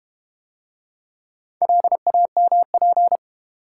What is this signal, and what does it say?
Morse code, heard as a keyed tone, about 31 wpm, 700 Hz. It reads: LAMP
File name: Easy Words - Speed-Racing with Hints - 25wpm